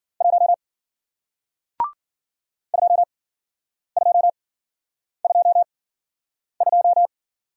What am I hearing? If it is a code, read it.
Morse code, heard as a keyed tone, about 54 wpm, spaced as for 9 wpm, 700 Hz. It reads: 1 2222